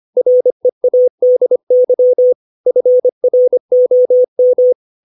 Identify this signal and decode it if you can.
Morse code, heard as a keyed tone, about 25 wpm, 500 Hz. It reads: READY FROM
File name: 632 words / 632 Words 25WPM